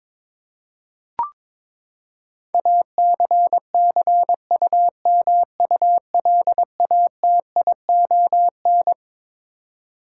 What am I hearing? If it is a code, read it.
Morse code, heard as a keyed tone, about 22 wpm, 700 Hz. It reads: ACCUMULATION
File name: Twelve Letter Words - 22wpm